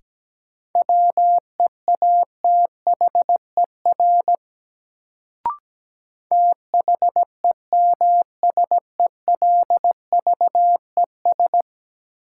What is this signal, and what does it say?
Morse code, heard as a keyed tone, about 17 wpm, 700 Hz. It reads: WEATHER THEMSELVES
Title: Sentences from Top 800 Words - New Word Drill - 17wpm